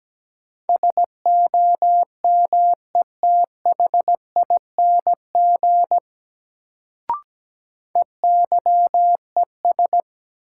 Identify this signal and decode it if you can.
Morse code, heard as a keyed tone, about 17 wpm, 700 Hz. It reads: SOMETHING EYES
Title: Sets of 5 Words - New Word Drill - 17wpm